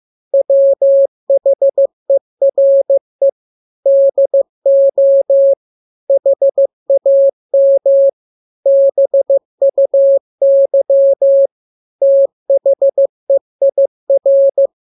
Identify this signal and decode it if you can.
Morse code, heard as a keyed tone, about 15 wpm, 550 Hz. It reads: WHERE DO HAM BUY THEIR
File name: jokes / joke 19 15 0000